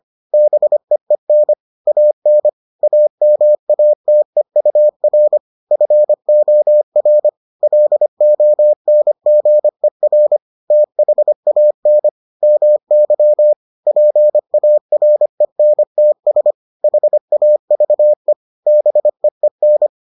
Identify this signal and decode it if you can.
Morse code, heard as a keyed tone, about 25 wpm, 600 Hz. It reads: BEEN AN AMATEUR FOR LONGER THAN MY PARENTS HAVE BEEN